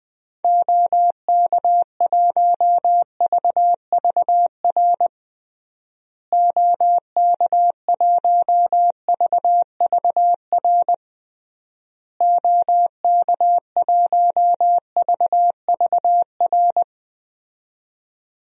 Morse code, heard as a keyed tone, about 20 wpm, 700 Hz. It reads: OK1VVR OK1VVR OK1VVR
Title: DXCC-100 Call Signs - 6 Characters - Repeated 3x - 20wpm